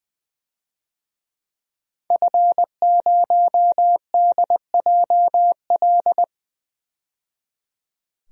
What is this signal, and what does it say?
Morse code, heard as a keyed tone, about 20 wpm, 700 Hz. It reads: F0DJL
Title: DXCC-100 Call Signs - 5 Characters - 20wpm